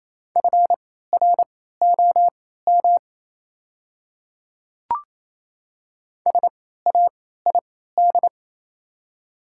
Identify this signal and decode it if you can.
Morse code, heard as a keyed tone, about 28 wpm, spaced as for 16 wpm, 700 Hz. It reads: FROM SAID
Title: 3x Character Spacing - Top 200 Words - 28wpm